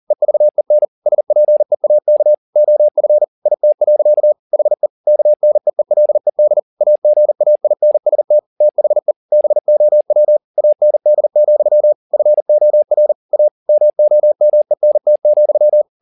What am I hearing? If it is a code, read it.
Morse code, heard as a keyed tone, about 40 wpm, 600 Hz. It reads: EVEN SPEAK OF IT. HE KNEELED AGAINST THE BOW AND, FOR A MOMENT,
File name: the old man and the sea 2 40wpm